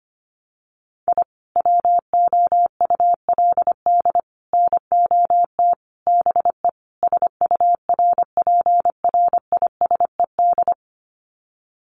Morse code, heard as a keyed tone, about 25 wpm, 700 Hz. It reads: I WOULD NOT BE SURPRISED
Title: Sets of 5 Words Encore2 25wpm